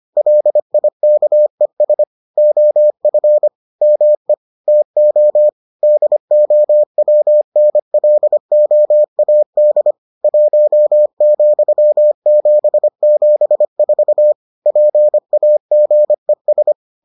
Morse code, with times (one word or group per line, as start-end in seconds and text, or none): 0.2-2.0 LIKES
2.4-3.5 OF
3.8-4.3 ME
4.7-5.5 TO
5.8-9.9 DOWNLOAD
10.2-14.3 1,774
14.7-16.7 PAGES